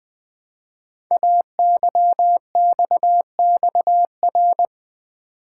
Morse code, heard as a keyed tone, about 20 wpm, 700 Hz. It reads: AYXXR
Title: Code Groups - 5 Letters - 20wpm